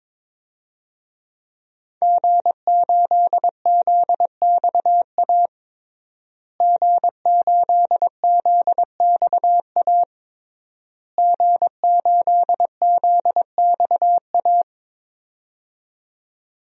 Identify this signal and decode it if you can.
Morse code, heard as a keyed tone, about 22 wpm, 700 Hz. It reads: G8ZXA G8ZXA G8ZXA
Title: DXCC-100 Call Signs - 5 Characters - Repeated 3x - 22wpm